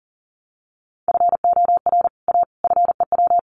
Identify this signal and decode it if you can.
Morse code, heard as a keyed tone, about 40 wpm, 700 Hz. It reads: FOR A FEW
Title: Sets of 3 Words 40wpm